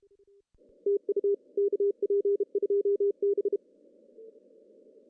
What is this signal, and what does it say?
Morse code, heard as a keyed tone, about 35 wpm, 410 Hz. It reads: TU KP2B
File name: CW / cqww-cw-k5tr